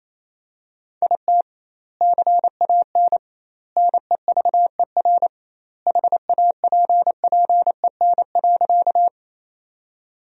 Morse code, heard as a keyed tone, about 28 wpm, 700 Hz. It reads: IT CAN NEVER HAPPEN.
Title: 2x Word-Spacing - Sentences from Top 400 Words - 28wpm